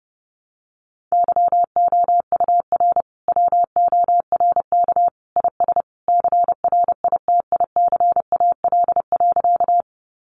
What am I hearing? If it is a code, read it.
Morse code, heard as a keyed tone, about 30 wpm, 700 Hz. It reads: YOUR WORK IS CRITICAL.
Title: Sentences from Top 1100 Words 30wpm